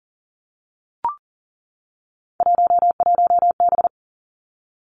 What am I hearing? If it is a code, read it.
Morse code, heard as a keyed tone, about 42 wpm, 700 Hz. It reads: 11B
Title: Three Challenging Letter-Number Combinations 40wpm